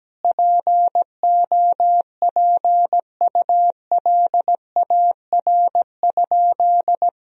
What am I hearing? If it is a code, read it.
Morse code, heard as a keyed tone, about 17 wpm, 700 Hz. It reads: POPULAR?
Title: Sentences from Top 1000 Words - Review Part 3 - 17wpm